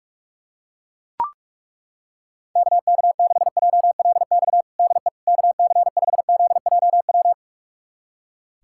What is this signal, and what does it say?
Morse code, heard as a keyed tone, about 41 wpm, 700 Hz. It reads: KK6JLX DE KC5ZJW
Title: USA Callsign DE Callsign 45wpm